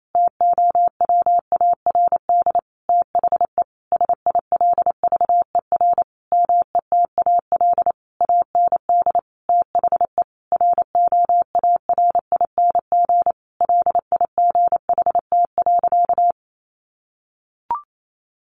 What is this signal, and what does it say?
Morse code, heard as a keyed tone, about 28 wpm, 700 Hz. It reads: TOWARD THE SILVER METAL AND THE ROARING LIGHT.
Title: A Sound Of Thunder 28wpm